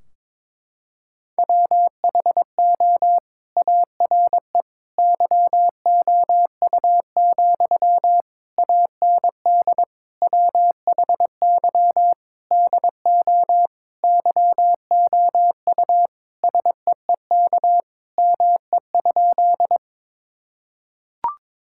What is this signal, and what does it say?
Morse code, heard as a keyed tone, about 22 wpm, 700 Hz. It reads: WHO ARE YOU, AND WHY DO YOU SEEK ME?
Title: Sentences from Top 1600 Words 22wpm